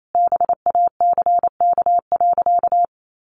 Morse code, heard as a keyed tone, about 28 wpm, 700 Hz. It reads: BACK.
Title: Sentences from Top 1500 Words 28wpm